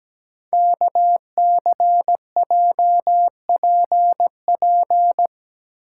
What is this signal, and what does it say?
Morse code, heard as a keyed tone, about 17 wpm, 700 Hz. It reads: KCJPP
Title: Code Groups - 5 Letters - 17wpm